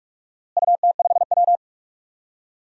Morse code, heard as a keyed tone, about 42 wpm, 700 Hz. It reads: UT5W
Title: DXCC-100 Call Signs - 4 Characters - Repeated 2x - 45wpm